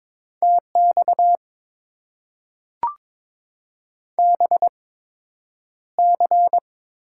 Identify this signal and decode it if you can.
Morse code, heard as a keyed tone, about 22 wpm, spaced as for 9 wpm, 700 Hz. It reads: TX BC
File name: Top 90 QSO Elements - 8x Character Spacing - 22wpm